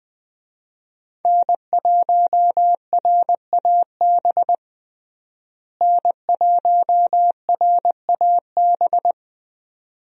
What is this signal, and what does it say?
Morse code, heard as a keyed tone, about 20 wpm, 700 Hz. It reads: N1RAB N1RAB
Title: USA Call Signs - 5 Characters - Repeated 2x - 20wpm